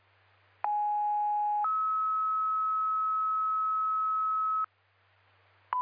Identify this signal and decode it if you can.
Two-tone page, heard as a one-second tone, then a three-second tone, about 830 then 1290 Hz, with noise. It is heard from 0.6 to 4.7 s.